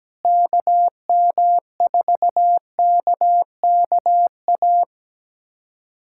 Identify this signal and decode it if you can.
Morse code, heard as a keyed tone, about 17 wpm, 700 Hz. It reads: KM4KKA
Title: USA Callsign DE Callsign - 6 Characters - 17wpm